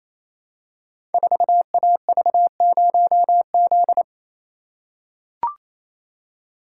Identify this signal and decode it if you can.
Morse code, heard as a keyed tone, about 29 wpm, 700 Hz. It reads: 4AV0Z
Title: Code Groups - 5 Letter:Numbers - 28wpm